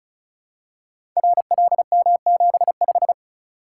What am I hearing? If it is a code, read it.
Morse code, heard as a keyed tone, about 36 wpm, 690 Hz. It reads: RLM75
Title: Code Groups - 5 Letter:Numbers - 35wpm